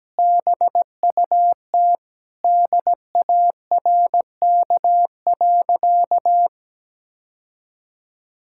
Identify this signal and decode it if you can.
Morse code, heard as a keyed tone, about 17 wpm, 700 Hz. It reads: BUT DARK.
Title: Sentences from Top 1600 Words 17wpm